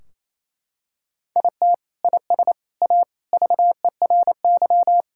Morse code, heard as a keyed tone, about 28 wpm, 700 Hz. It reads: IT IS A VERY
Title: Sentences from Top 1600 Words 28wpm